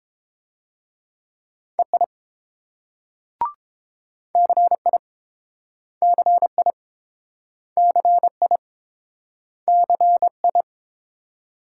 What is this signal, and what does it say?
Morse code, heard as a keyed tone, about 29 wpm, 700 Hz. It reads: EI CI CI CI CI